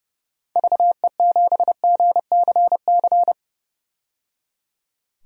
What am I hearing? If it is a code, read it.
Morse code, heard as a keyed tone, about 30 wpm, 700 Hz. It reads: VE7GCC
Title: DXCC-100 Call Signs - 6 Characters - 30wpm